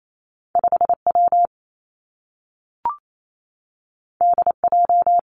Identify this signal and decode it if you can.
Morse code, heard as a keyed tone, about 28 wpm, 700 Hz. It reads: 5 W D J
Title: Two Challenging Letter-Number Combinations 28wpm